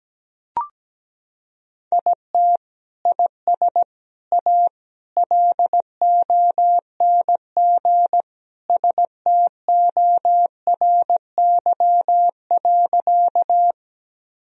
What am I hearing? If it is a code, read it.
Morse code, heard as a keyed tone, about 17 wpm, 700 Hz. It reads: IT IS A LONG STORY.
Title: Sentences from Top 1000 Words - Review Part 1 - 17wpm